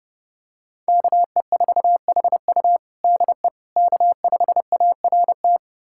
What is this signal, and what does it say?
Morse code, heard as a keyed tone, about 32 wpm, 700 Hz. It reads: KE4HU DE K5ART
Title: USA Callsign DE Callsign - 5 Characters - 30wpm